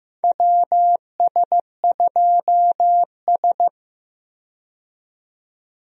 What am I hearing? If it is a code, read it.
Morse code, heard as a keyed tone, about 15 wpm, 700 Hz. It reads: WS2S
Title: USA Call Signs - 4 Characters - 15wpm